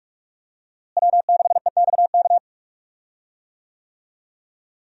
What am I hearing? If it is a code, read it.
Morse code, heard as a keyed tone, about 42 wpm, 700 Hz. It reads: W6EXK
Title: USA Call Signs - 5 Characters - Repeated 2x - 45wpm